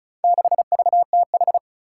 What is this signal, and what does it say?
Morse code, heard as a keyed tone, about 34 wpm, 700 Hz. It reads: 6VTH
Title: Four Challenging Letter-Number Combinations 35wpm